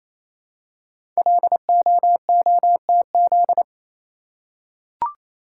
Morse code, heard as a keyed tone, about 27 wpm, 700 Hz. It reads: LOOTZ
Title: Code Groups - 5 Letters - 28wpm